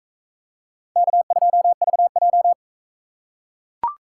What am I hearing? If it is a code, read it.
Morse code, heard as a keyed tone, about 42 wpm, 700 Hz. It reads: K2VJ